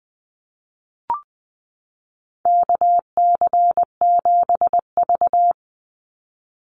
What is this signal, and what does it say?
Morse code, heard as a keyed tone, about 20 wpm, 700 Hz. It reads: KC7V
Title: Call Signs - 20wpm